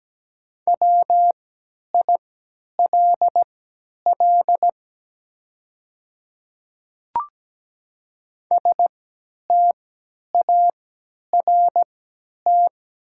Morse code, heard as a keyed tone, about 17 wpm, spaced as for 10 wpm, 700 Hz. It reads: WILL START
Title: 3x Character Spacing - Top 300 Words - 17wpm